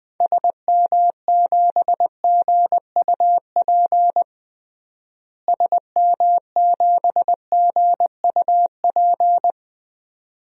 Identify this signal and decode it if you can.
Morse code, heard as a keyed tone, about 21 wpm, 700 Hz. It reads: SM7GUP SM7GUP
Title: DXCC-100 Call Signs - 6 Characters - Repeated 2x - 20wpm